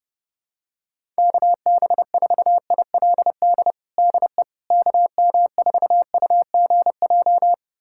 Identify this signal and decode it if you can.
Morse code, heard as a keyed tone, about 32 wpm, 700 Hz. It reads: KB4ILD DE KM4UGJ
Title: USA Callsign DE Callsign - 6 Characters - 30wpm